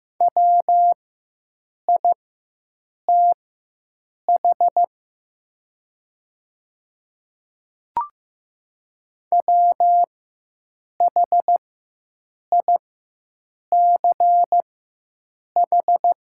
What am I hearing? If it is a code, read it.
Morse code, heard as a keyed tone, about 15 wpm, spaced as for 7 wpm, 700 Hz. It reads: WITH WHICH
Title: 4x Character Spacing - Top 100 Words - 15wpm